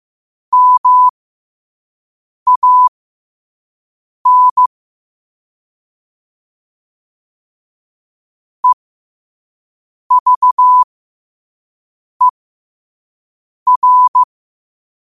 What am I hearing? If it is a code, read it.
Morse code, heard as a keyed tone, about 15 wpm, spaced as for 5 wpm, 1000 Hz. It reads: MAN EVER
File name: qotdfiles / 5wpm